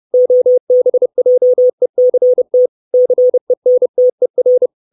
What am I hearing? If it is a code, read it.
Morse code, heard as a keyed tone, about 30 wpm, 500 Hz. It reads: OBJECT CENTER
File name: 632 Words 30WPM